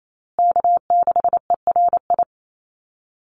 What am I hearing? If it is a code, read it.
Morse code, heard as a keyed tone, about 28 wpm, 700 Hz. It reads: K6ERI
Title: USA Call Signs Repeated 3x 28wpm